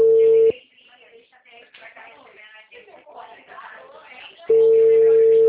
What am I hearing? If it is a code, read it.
Ringback tone (China, 450 Hz), with speech.